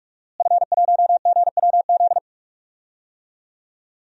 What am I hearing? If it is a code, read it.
Morse code, heard as a keyed tone, about 46 wpm, 700 Hz. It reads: F1GWZ
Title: DXCC-100 Call Signs - 5 Characters - Repeated 2x - 45wpm